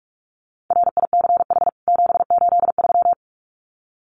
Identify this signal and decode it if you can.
Morse code, heard as a keyed tone, about 44 wpm, 700 Hz. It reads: RICH 783
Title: CWT Exchange 45wpm